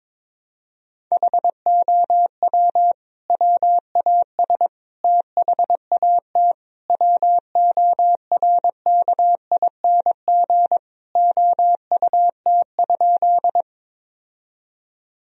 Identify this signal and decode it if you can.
Morse code, heard as a keyed tone, about 22 wpm, 700 Hz. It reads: HOW WAS THAT WORKING OUT?
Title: Sentences from Top 1000 Words - Review Part 4 - 22wpm